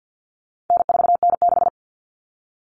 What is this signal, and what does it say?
Morse code, heard as a keyed tone, about 50 wpm, 700 Hz. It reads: N4N6
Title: Common Cut Numbers - 4-Digit 50wpm